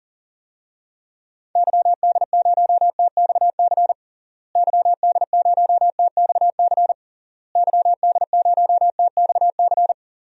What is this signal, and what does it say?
Morse code, heard as a keyed tone, about 40 wpm, 700 Hz. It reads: YD0TXC YD0TXC YD0TXC
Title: DXCC-100 Call Signs - 6 Characters - Repeated 3x - 40wpm